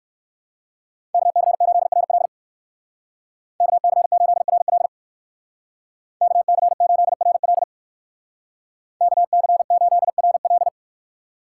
Morse code, heard as a keyed tone, about 59 wpm, 700 Hz. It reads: KC8RL KC8RL KC8RL KC8RL